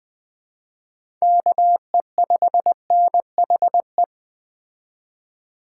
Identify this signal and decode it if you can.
Morse code, heard as a keyed tone, about 21 wpm, 700 Hz. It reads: KE5NHE
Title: Speed Racing - USA 2x3 Call Signs 17wpm